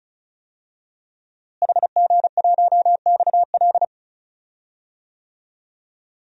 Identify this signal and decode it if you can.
Morse code, heard as a keyed tone, about 34 wpm, 700 Hz. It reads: HG1XL
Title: DXCC-100 Call Signs - 5 Characters - Repeated 2x - 35wpm